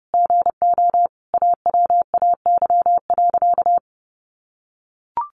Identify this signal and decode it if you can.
Morse code, heard as a keyed tone, about 30 wpm, 700 Hz. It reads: GO AWAY.
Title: Sentences from Top 1500 Words 30wpm